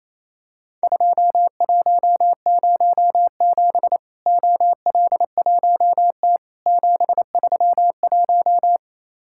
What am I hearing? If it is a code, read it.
Morse code, heard as a keyed tone, about 28 wpm, 700 Hz. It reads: 2107 OL1T 731
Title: Worked All Europe DX Contest - QTC Exchange - 28wpm